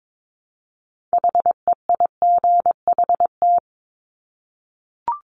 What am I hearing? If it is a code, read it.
Morse code, heard as a keyed tone, about 22 wpm, 700 Hz. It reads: HEIGHT
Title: Multi-letter Phonemes 22wpm